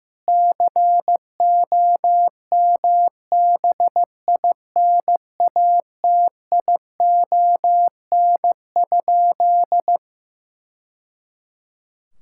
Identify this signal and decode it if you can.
Morse code, heard as a keyed tone, about 15 wpm, 700 Hz. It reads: COMBINATION?
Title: Sentences from Top 1000 Words - Review Part 6 - 15wpm